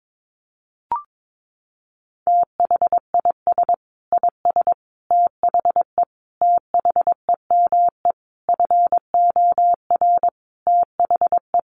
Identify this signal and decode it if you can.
Morse code, heard as a keyed tone, about 22 wpm, 700 Hz. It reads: THIS IS THE THEME FOR THE
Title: Sentences from Top 1100 Words 22wpm